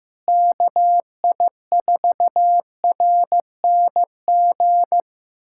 Morse code, heard as a keyed tone, about 15 wpm, 700 Hz. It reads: KI4RNG
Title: USA Callsign DE Callsign 15wpm